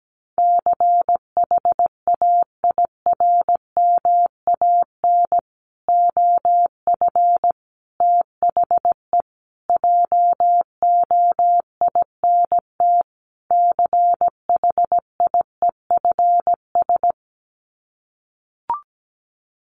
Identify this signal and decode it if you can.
Morse code, heard as a keyed tone, about 17 wpm, 700 Hz. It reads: CHAIRMAN OF THE JOINT CHIEFS
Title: Sets of 5 Words Encore 17wpm